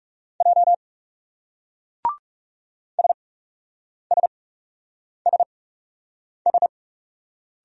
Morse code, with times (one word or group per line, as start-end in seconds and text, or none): 0.4-0.7 J
3.0-6.7 SSSS